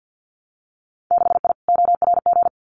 Morse code, heard as a keyed tone, about 50 wpm, 700 Hz. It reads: 6I ORG
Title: Winter Field Day Exchanges Encore 50wpm